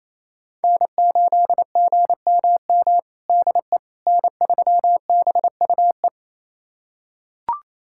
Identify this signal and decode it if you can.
Morse code, heard as a keyed tone, about 28 wpm, 700 Hz. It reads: N8GMM DE N3BUE
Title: USA Callsign DE Callsign - 5 Characters - 28wpm